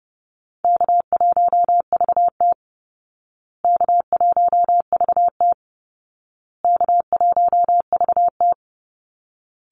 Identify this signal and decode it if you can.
Morse code, heard as a keyed tone, about 30 wpm, 700 Hz. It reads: K1VT K1VT K1VT
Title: Easy Call Signs Repeated 3x - 30wpm